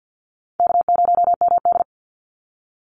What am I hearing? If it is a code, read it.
Morse code, heard as a keyed tone, about 50 wpm, 700 Hz. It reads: K0MD
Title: Easy Call Signs Repeated 2x - 50wpm